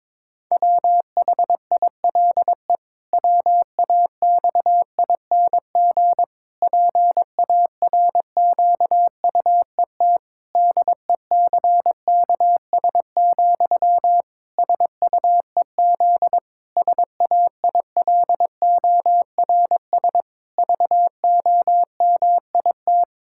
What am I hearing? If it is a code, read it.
Morse code, heard as a keyed tone, about 22 wpm, 700 Hz. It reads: WHILE WAXING PARQUET DECKS, SUEZ SAILORS VOMIT
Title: Pangram Challenge v2 - 22wpm